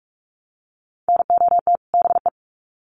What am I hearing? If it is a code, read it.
Morse code, heard as a keyed tone, about 46 wpm, 700 Hz. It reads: NOT BE
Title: Sets of 2 Words 45wpm